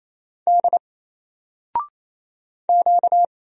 Morse code, heard as a keyed tone, about 27 wpm, 700 Hz. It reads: D Q
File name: Single-Letter-Number-28wpm